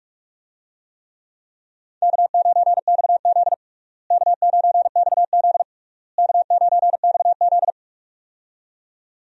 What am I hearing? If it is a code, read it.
Morse code, heard as a keyed tone, about 46 wpm, 700 Hz. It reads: K9XZ K9XZ K9XZ
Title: USA Call Signs - 4 Characters - Repeated 3x - 45wpm